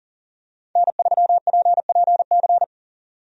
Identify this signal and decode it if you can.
Morse code, heard as a keyed tone, about 38 wpm, 700 Hz. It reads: N3PPC
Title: USA Call Signs - 5 Characters - 40wpm